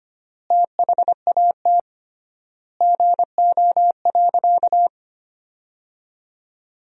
Morse code, heard as a keyed tone, about 25 wpm, 700 Hz. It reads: THAT GO.
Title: 3x Word-Spacing - Sentences from Top 300 Words - 25wpm